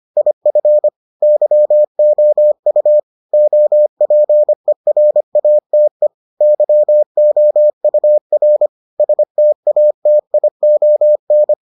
Morse code, heard as a keyed tone, about 25 wpm, 600 Hz. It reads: IF YOU OPERATE YOUR STATION